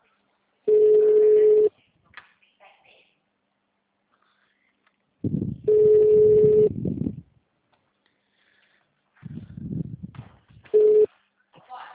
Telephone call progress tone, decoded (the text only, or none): ringback tone